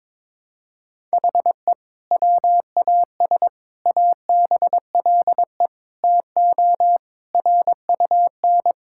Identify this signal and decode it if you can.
Morse code, heard as a keyed tone, about 21 wpm, 700 Hz. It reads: HE WAS ABLE TO RUN
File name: practice / 22wpm / Sentences with 2-Word Phrases - 22wpm